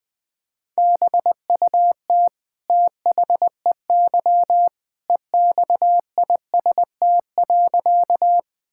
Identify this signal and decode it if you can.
Morse code, heard as a keyed tone, about 21 wpm, 700 Hz. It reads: BUT THEY EXIST.
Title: Sentences from Top 1000 Words - Review Part 5 - 20wpm